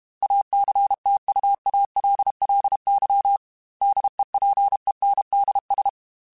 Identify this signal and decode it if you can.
Morse code, heard as a keyed tone, about 32 wpm, 800 Hz. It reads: ACTUALLY DEPENDS
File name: Technical Subjects / Technical Subjects-mp3s / Sw Pwr Sup @30